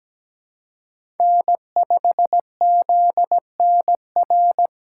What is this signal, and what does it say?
Morse code, heard as a keyed tone, about 17 wpm, 700 Hz. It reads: N5ZNR
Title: USA Call Signs - 5 Characters - 2x Character Spacing - 17wpm